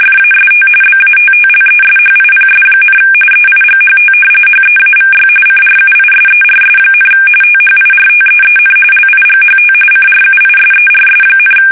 RTTY, about 45.45 baud. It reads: FAST BATTLESHIPS, EIGHT HEAVY CRUISERS, THIRTEEN LIGHT CRUISERS,
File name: fsk850 / battleshipnj